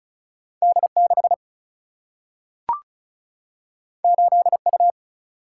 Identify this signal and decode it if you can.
Morse code, heard as a keyed tone, about 36 wpm, 700 Hz. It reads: D6 8U
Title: Single-Letter Single-Number 35wpm